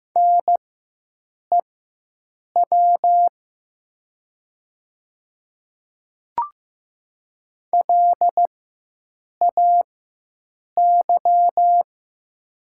Morse code, heard as a keyed tone, about 15 wpm, spaced as for 7 wpm, 700 Hz. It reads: NEW LAY
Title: 4x Character Spacing - Top 500 Words - 15wpm